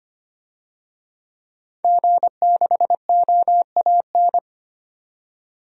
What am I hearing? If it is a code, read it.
Morse code, heard as a keyed tone, about 25 wpm, 700 Hz. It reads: G6OAN
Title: DXCC-100 Call Signs - 5 Characters - 25wpm